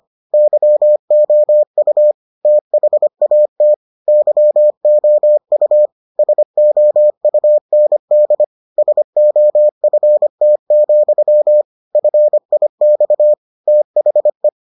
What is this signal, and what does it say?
Morse code, heard as a keyed tone, about 25 wpm, 600 Hz. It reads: YOU THAT YOU SOUND SOFT, FIX THE